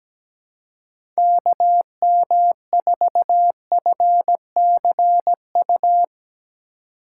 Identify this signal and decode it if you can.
Morse code, heard as a keyed tone, about 17 wpm, 700 Hz. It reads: KM4FCU
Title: USA Call Signs - 6 Characters - 3x Character Spacing - 17wpm